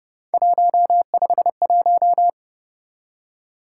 Morse code, heard as a keyed tone, about 29 wpm, 700 Hz. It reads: 151